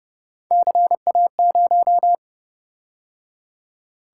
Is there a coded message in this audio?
Yes: Morse code from 0.5 to 2.2 s, heard as a keyed tone, about 29 wpm, 700 Hz. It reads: CA0